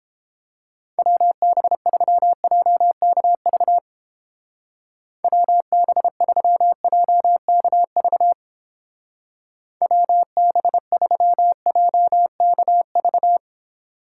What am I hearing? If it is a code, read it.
Morse code, heard as a keyed tone, about 31 wpm, 700 Hz. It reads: WB3JKV WB3JKV WB3JKV